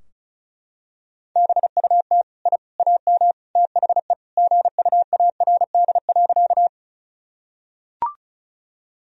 Morse code, heard as a keyed tone, about 36 wpm, 700 Hz. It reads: BUT I AM THE GUARD.
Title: Sentences from Top 1000 Words - Review Part 4 - 35wpm